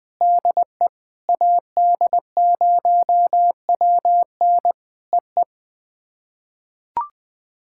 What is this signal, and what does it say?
Morse code, heard as a keyed tone, about 20 wpm, 700 Hz. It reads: DE AD0WN EE
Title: POTA QSO Practice - 20wpm